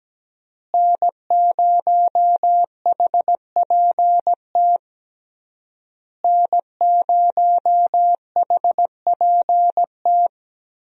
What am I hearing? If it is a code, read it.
Morse code, heard as a keyed tone, about 17 wpm, 700 Hz. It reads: N0HPT N0HPT
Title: USA Call Signs - 5 Characters - Repeated 3x - 17wpm